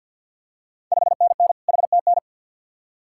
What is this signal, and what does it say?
Morse code, heard as a keyed tone, about 51 wpm, 700 Hz. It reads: 5NN HTN
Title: Ontario QSO Party Exchange 50wpm